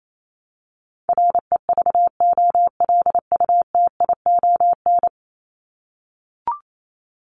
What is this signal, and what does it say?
Morse code, heard as a keyed tone, about 28 wpm, 700 Hz. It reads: REVOLUTION
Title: Top 1751-2000 Words 28wpm